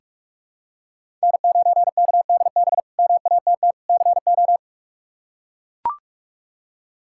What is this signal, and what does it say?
Morse code, heard as a keyed tone, about 43 wpm, 700 Hz. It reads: N9KDB MATT CO
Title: K1USN SST QSO Practice - 45wpm